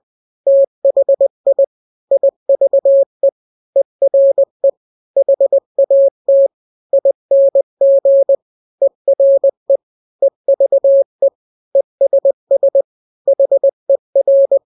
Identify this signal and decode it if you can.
Morse code, heard as a keyed tone, about 20 wpm, 550 Hz. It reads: THI IVE ERE HAT ING ERE EVE ESS HER
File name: n-grams / 3-grams 20 18 WPM